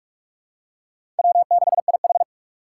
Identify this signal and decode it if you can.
Morse code, heard as a keyed tone, about 46 wpm, 700 Hz. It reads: W6IH
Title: USA Call Signs - 4 Characters - 4x Character Spacing - 45wpm